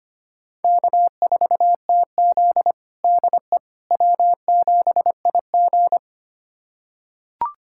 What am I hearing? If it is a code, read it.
Morse code, heard as a keyed tone, about 25 wpm, 700 Hz. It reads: K4TZ DE W7IG
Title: USA Callsign DE Callsign - 4 Characters - 25wpm